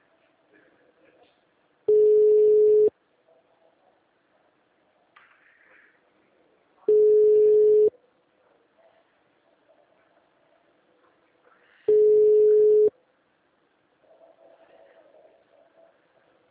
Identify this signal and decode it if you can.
Ringback tone (Europe, 425 Hz), with speech.